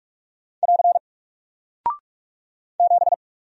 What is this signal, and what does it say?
Morse code, heard as a keyed tone, about 44 wpm, 700 Hz. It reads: <AR> 7
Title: Single Character (Letter:Number:Punctuation:Prosign) Alt 45wpm